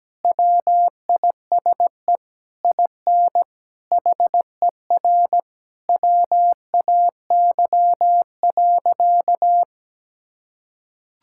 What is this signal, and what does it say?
Morse code, heard as a keyed tone, about 17 wpm, 700 Hz. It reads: WISE IN HER WAY.
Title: Sentences from Top 1600 Words 17wpm